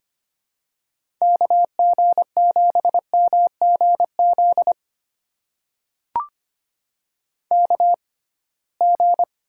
Morse code, heard as a keyed tone, about 25 wpm, spaced as for 19 wpm, 700 Hz. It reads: KG7MGZ KG
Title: USA Call Signs - 6x Character Spacing - 25wpm